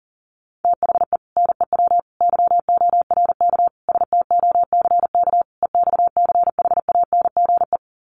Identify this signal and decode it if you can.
Morse code, heard as a keyed tone, about 40 wpm, 700 Hz. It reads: THE NEW YORK STOCK EXCHANGE
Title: Sets of 5 Words 40wpm